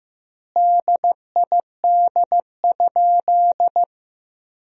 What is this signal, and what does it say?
Morse code, heard as a keyed tone, about 15 wpm, 700 Hz. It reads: DID?
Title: 2x Word-Spacing - Sentences from Top 100 Words - 15wpm